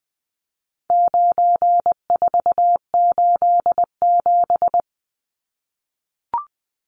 Morse code, heard as a keyed tone, about 20 wpm, 700 Hz. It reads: 9487
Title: Numbers - 4-Digit 20wpm